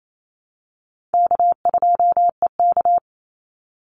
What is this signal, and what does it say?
Morse code, heard as a keyed tone, about 28 wpm, 700 Hz. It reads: K2EK
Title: Easy Call Signs - 28wpm